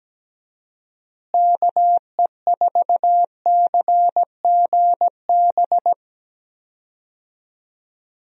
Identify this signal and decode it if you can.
Morse code, heard as a keyed tone, about 17 wpm, 700 Hz. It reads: KE4CGB
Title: USA Call Signs - 6 Characters - Repeated 3x - 17wpm